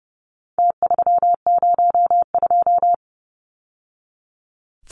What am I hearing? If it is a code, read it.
Morse code, heard as a keyed tone, about 30 wpm, 700 Hz. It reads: T302
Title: Numbers - Contest Style - 4-Digit 30wpm